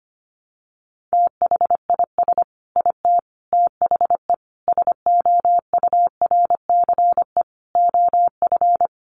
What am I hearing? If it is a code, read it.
Morse code, heard as a keyed tone, about 25 wpm, 700 Hz. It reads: THIS IT THE SOURCE OF
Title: Sentences from Top 800 Words 25wpm